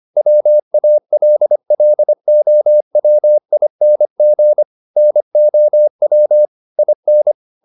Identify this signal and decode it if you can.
Morse code, heard as a keyed tone, about 25 wpm, 600 Hz. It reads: WALLOWING NOW IN